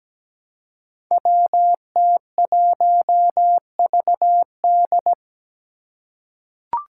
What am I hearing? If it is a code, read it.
Morse code, heard as a keyed tone, about 17 wpm, 700 Hz. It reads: WT1VD
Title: Five Challenging Letter-Number Combinations 17wpm